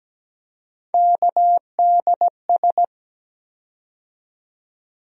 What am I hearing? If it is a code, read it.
Morse code, heard as a keyed tone, about 17 wpm, 700 Hz. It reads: KDS
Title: USA Call Sign Postfixes - 3 Letters 17wpm